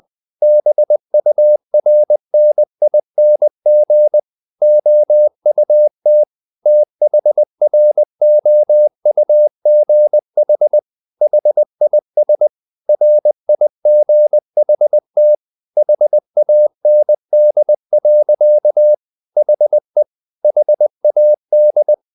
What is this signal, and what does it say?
Morse code, heard as a keyed tone, about 20 wpm, 600 Hz. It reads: BURNING OUT THROUGH HIS RIGHT HAND. HE HAD
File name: the old man and the sea part 2 20wpm